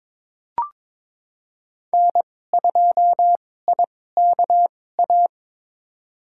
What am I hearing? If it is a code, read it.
Morse code, heard as a keyed tone, about 22 wpm, spaced as for 16 wpm, 700 Hz. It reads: N2IKA
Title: USA Call Signs - 5 Characters - 2x Character Spacing - 22wpm